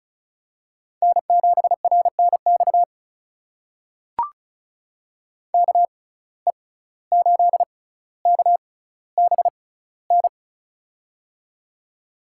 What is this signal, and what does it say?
Morse code, heard as a keyed tone, about 36 wpm, spaced as for 17 wpm, 700 Hz. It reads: N7RNX KE8KBN
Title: USA Call Signs - 6x Character Spacing - 35wpm